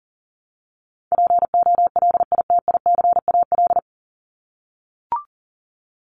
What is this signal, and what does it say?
Morse code, heard as a keyed tone, about 40 wpm, 700 Hz. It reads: POLITICAL
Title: Top 501-750 Words 40wpm